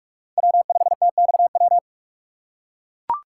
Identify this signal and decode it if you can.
Morse code, heard as a keyed tone, about 42 wpm, 700 Hz. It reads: W5TXW
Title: USA Call Signs - 5 Characters - 3x Character Spacing - 45wpm